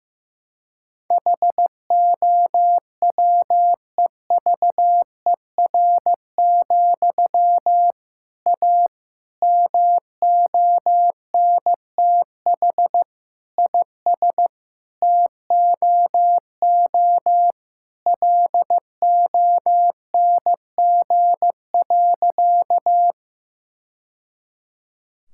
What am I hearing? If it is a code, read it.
Morse code, heard as a keyed tone, about 15 wpm, 700 Hz. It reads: HOWEVER, A MONTH IS TOO LONG.
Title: Sentences from Top 1000 Words - Review Part 5 - 15wpm